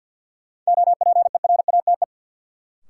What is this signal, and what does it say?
Morse code, heard as a keyed tone, about 53 wpm, 700 Hz. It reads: OPERATE
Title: Sentences from Top 1500 Words - New Word Drill - 50wpm